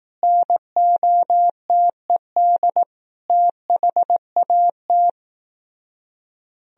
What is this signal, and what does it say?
Morse code, heard as a keyed tone, about 18 wpm, 700 Hz. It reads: NOTED THAT
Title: Speed-Racing - Sets of 5 Words 15wpm